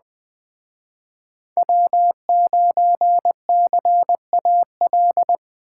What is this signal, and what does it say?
Morse code, heard as a keyed tone, about 20 wpm, 700 Hz. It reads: W9CAL